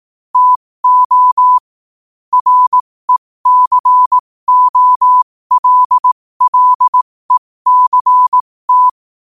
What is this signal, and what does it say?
Morse code, heard as a keyed tone, about 18 wpm, spaced as for 15 wpm, 1000 Hz. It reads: TO RECOLLECT
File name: qotdfiles / 15wpm